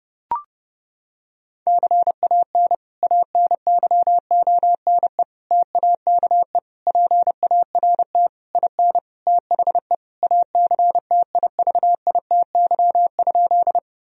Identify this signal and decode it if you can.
Morse code, heard as a keyed tone, about 30 wpm, 700 Hz. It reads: CAN ANYONE TAKE PART IN THE ACTIVITY?
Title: Sentences from Top 1000 Words - Review Part 4 - 30wpm